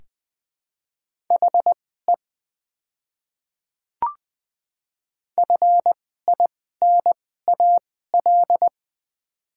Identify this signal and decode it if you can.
Morse code, heard as a keyed tone, about 21 wpm, spaced as for 15 wpm, 690 Hz. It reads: HE FINAL
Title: 2x Character Spacing - Top 500 Words - 20wpm